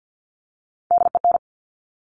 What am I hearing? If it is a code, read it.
Morse code, heard as a keyed tone, about 50 wpm, 700 Hz. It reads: DEN
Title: Top Three Letter Combinations in Words 50wpm